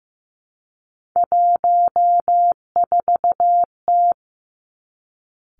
Morse code, heard as a keyed tone, about 15 wpm, 700 Hz. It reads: 14T
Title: Three Challenging Letter-Number Combinations 15wpm